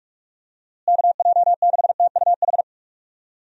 Morse code, heard as a keyed tone, about 46 wpm, 700 Hz. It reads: KJ6TUH
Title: USA Call Signs - 6 Characters - Repeated 3x - 45wpm